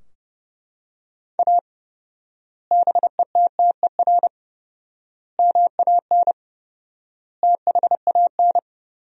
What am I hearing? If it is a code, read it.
Morse code, heard as a keyed tone, about 30 wpm, 700 Hz. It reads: A BETTER MAN THAN
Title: 4x Word-Spacing - Sentences from Top 400 Words - 30wpm